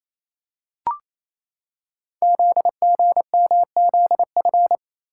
Morse code, heard as a keyed tone, about 29 wpm, 700 Hz. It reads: ZGMZF
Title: Code Groups - 5 Letters - 28wpm